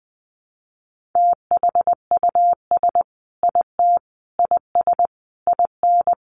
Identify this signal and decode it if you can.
Morse code, heard as a keyed tone, about 19 wpm, 700 Hz. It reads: THUS IT IS IN
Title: Sentences from Top 800 Words 20wpm